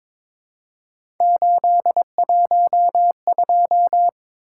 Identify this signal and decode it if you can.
Morse code, heard as a keyed tone, about 22 wpm, 700 Hz. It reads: 812
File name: practice / 22wpm / Three Letter-Number Combinations - 22wpm